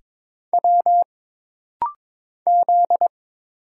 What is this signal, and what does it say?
Morse code, heard as a keyed tone, about 22 wpm, 700 Hz. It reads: W Z
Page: This rapid-fire timing is geared to push your ICR (Instant Character Recognition) skills.